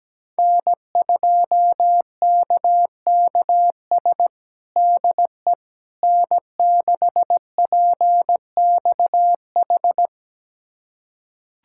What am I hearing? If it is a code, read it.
Morse code, heard as a keyed tone, about 17 wpm, 700 Hz. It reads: N2KKS DE N6PXH
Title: USA Callsign DE Callsign - 5 Characters - 17wpm